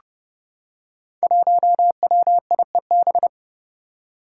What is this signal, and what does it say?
Morse code, heard as a keyed tone, about 30 wpm, 700 Hz. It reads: 1WIEB
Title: Five Challenging Letter-Number Combinations 30wpm